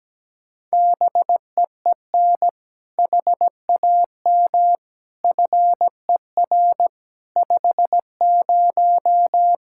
Morse code, heard as a keyed tone, about 17 wpm, 700 Hz. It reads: BEEN HAM FER 50
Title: Practice QSO - Exchange 3 - More Info - 17wpm